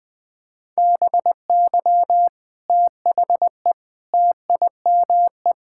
Morse code, heard as a keyed tone, about 20 wpm, 700 Hz. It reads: BY THE TIME
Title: Sentences with 4-Word Phrases - Spoken Beginning - 20wpm